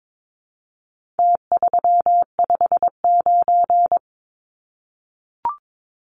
Morse code, heard as a keyed tone, about 22 wpm, 700 Hz. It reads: T359
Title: Numbers - Contest Style - 4-Digit 22wpm